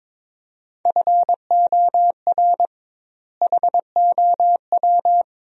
Morse code, heard as a keyed tone, about 22 wpm, 700 Hz. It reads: FOR HOW
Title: 2x Word-Spacing - Sentences from Top 200 Words - 22wpm